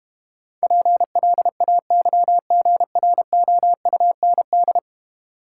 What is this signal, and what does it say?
Morse code, heard as a keyed tone, about 32 wpm, 700 Hz. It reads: PLAYGROUND